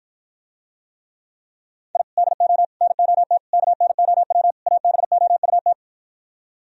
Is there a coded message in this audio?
Yes: Morse code from 1.9 to 5.7 s, heard as a keyed tone, about 52 wpm, 700 Hz. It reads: I DO NOT KNOW ABOUT